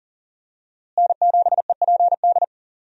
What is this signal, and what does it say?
Morse code, heard as a keyed tone, about 40 wpm, 700 Hz. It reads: N7EPD
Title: USA Call Signs - 5 Characters - Repeated 2x - 40wpm